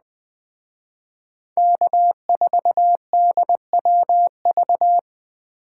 Morse code, heard as a keyed tone, about 20 wpm, 700 Hz. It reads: K4DWV
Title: Speed Racing - USA 1x3 Call Signs 17wpm